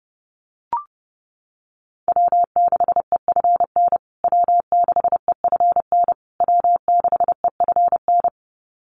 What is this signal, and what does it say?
Morse code, heard as a keyed tone, about 30 wpm, 700 Hz. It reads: W6EFN W6EFN W6EFN
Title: USA Call Signs Repeated 3x 30wpm